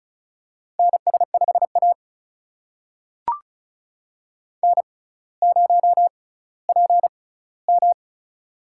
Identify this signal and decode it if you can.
Morse code, heard as a keyed tone, about 36 wpm, spaced as for 18 wpm, 700 Hz. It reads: NS5A N0PM
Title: USA Call Signs - 4 Characters - 6x Character Spacing - 35wpm